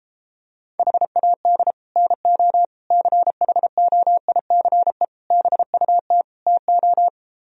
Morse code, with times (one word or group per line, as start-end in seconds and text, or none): 0.8-1.7 HAD
2.0-2.6 NO
2.9-5.0 CHOICE
5.3-6.2 BUT
6.5-7.1 TO